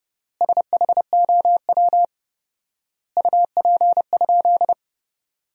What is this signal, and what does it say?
Morse code, heard as a keyed tone, about 31 wpm, 700 Hz. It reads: SHOW UP?
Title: 4x Word-Spacing - Sentences from Top 200 Words - 30wpm